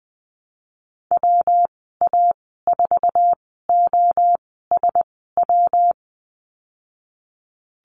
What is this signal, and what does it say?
Morse code, heard as a keyed tone, about 20 wpm, spaced as for 15 wpm, 700 Hz. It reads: WA4OSW
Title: 207 Lesson 33g - Letter j - Call Signs 20wpm